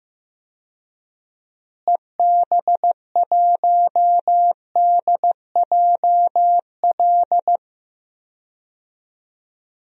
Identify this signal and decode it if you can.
Morse code, heard as a keyed tone, about 15 wpm, 700 Hz. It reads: EB1DJL
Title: DXCC-100 Call Signs - 6 Characters - Repeated 2x - 15wpm